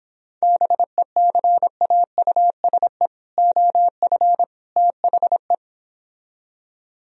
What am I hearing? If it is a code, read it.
Morse code, heard as a keyed tone, about 29 wpm, 700 Hz. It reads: BECAUSE OF THE